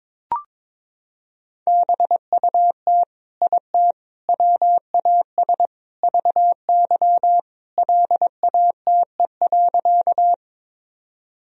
Morse code, heard as a keyed tone, about 22 wpm, 700 Hz. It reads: BUT IT WAS VY LATE.
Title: Sentences from Top 300 Words and Abbreviations 22wpm